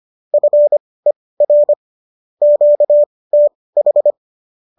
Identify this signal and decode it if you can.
Morse code, heard as a keyed tone, about 25 wpm, spaced as for 18 wpm, 590 Hz. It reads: FER QTH
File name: qso218 18